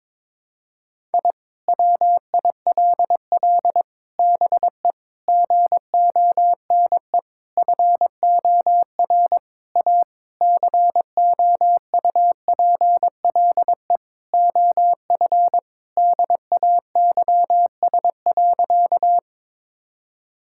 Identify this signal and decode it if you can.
Morse code, heard as a keyed tone, about 22 wpm, 700 Hz. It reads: I WILL BE GONE FOR A COUPLE OF DAYS.
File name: practice / 22wpm / Sentences with 4-Word Phrases - 22wpm